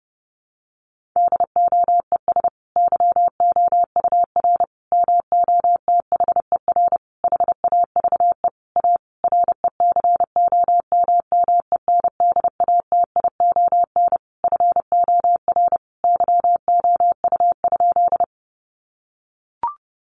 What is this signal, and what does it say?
Morse code, heard as a keyed tone, about 30 wpm, 700 Hz. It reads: DOES YOUR MOTHER HAVE A RECOMMENDATION FOR YOU?
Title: Sentences from Top 1400 Words 30wpm